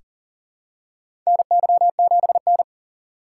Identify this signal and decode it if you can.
Morse code, heard as a keyed tone, about 40 wpm, 700 Hz. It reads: NY7N
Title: K1USN SST QSO Practice - 40wpm